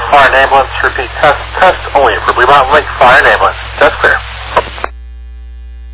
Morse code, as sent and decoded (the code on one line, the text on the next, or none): none